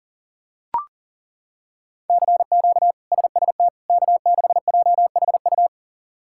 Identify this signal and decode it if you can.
Morse code, heard as a keyed tone, about 40 wpm, 700 Hz. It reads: CQ SST K6JHU
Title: K1USN SST QSO Practice - 40wpm